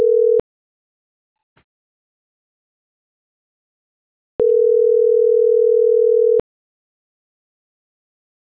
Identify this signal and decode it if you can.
Ringback tone (North America), 440 + 480 Hz.